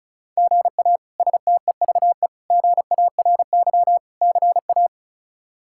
Morse code, heard as a keyed tone, about 36 wpm, 700 Hz. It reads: GA STEVE GARY CA
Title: K1USN SST QSO Practice - 35wpm